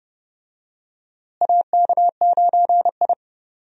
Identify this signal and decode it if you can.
Morse code, heard as a keyed tone, about 30 wpm, 700 Hz. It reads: AK9I
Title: USA Call Signs - 4 Characters - Repeated 3x - 30wpm